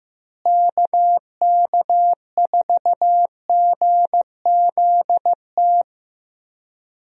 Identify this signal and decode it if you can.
Morse code, heard as a keyed tone, about 15 wpm, 700 Hz. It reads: KK4GZT
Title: USA Call Signs - 6 Characters - Repeated 3x - 15wpm